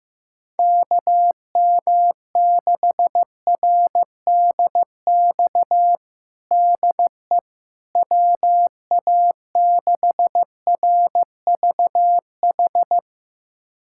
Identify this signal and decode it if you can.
Morse code, heard as a keyed tone, about 15 wpm, 700 Hz. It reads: KM6RDX DE WA6RVH
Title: USA Callsign DE Callsign - 6 Characters - 15wpm